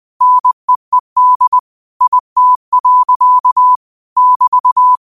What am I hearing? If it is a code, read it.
Morse code, heard as a keyed tone, about 20 wpm, 1000 Hz. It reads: NEED IT. <BT>
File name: qotdfiles / 20wpm